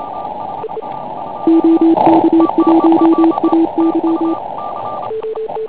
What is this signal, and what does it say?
Morse code, heard as a keyed tone, about 27 wpm, 330 Hz, with radio noise. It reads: OK1AY
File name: Pile UP CUC204